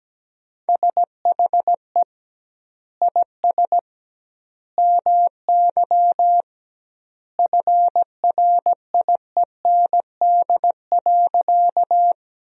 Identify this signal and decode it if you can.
Morse code, heard as a keyed tone, about 17 wpm, 700 Hz. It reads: SHE IS MY FRIEND.
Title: 2x Word-Spacing - Sentences from Top 300 Words - 17wpm